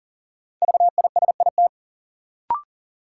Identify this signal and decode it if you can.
Morse code, heard as a keyed tone, about 40 wpm, 700 Hz. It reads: VISIT
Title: Sentences from Top 900 Words - New Word Drill - 40wpm